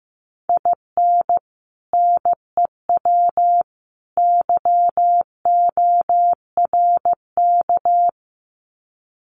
Morse code, heard as a keyed tone, about 15 wpm, 700 Hz. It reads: IN NEW YORK
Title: Sets of 3 Words 15wpm